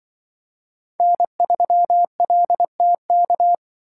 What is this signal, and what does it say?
Morse code, heard as a keyed tone, about 25 wpm, 700 Hz. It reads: N3LTK
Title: Speed Racing - USA 1x3 Call Signs 20wpm